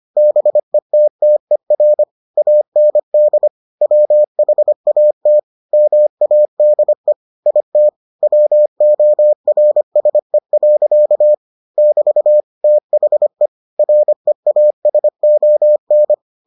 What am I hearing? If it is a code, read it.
Morse code, heard as a keyed tone, about 25 wpm, 600 Hz. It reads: BETTER AND WHAT MADE IT WORSE. <BT> THE REASON